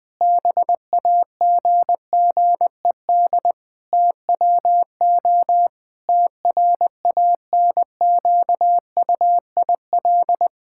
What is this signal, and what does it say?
Morse code, heard as a keyed tone, about 20 wpm, 700 Hz. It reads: BAGGED TWO TRANQUIL
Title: Pangram Challenge v2 - 20wpm